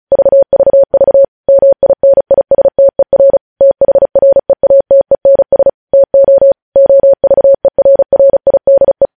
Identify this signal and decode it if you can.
Morse code, heard as a keyed tone, about 36 wpm, 550 Hz. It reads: VVV MINISTER THREATENS TO OVERRIDE
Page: News headlines in Morse code at 35 wpm